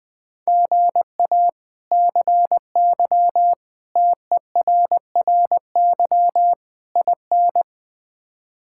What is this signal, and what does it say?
Morse code, heard as a keyed tone, about 20 wpm, 700 Hz. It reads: GA CY TERRY IN
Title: K1USN SST QSO Practice - 20wpm